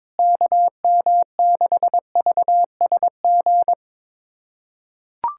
Morse code, heard as a keyed tone, about 22 wpm, 700 Hz. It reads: KM6VSG
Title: USA Callsign DE Callsign 22wpm